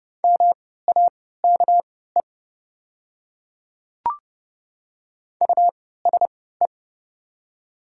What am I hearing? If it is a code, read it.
Morse code, heard as a keyed tone, about 30 wpm, spaced as for 17 wpm, 700 Hz. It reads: MAKE USE
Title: 3x Character Spacing - Top 200 Words - 30wpm